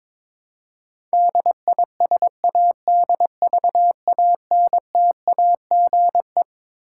Morse code, heard as a keyed tone, about 22 wpm, 700 Hz. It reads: DISADVANTAGE
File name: Twelve Letter Words - 22wpm